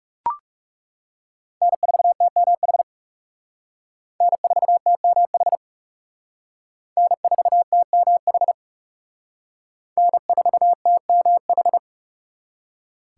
Morse code, heard as a keyed tone, about 36 wpm, 700 Hz. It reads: N4TMH N4TMH N4TMH N4TMH
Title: Speed Racing - USA Call Signs 30wpm